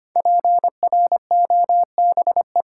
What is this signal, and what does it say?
Morse code, heard as a keyed tone, about 26 wpm, 700 Hz. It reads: PROBE
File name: 5 Letter Words - First 3 Letters Given - 25wpm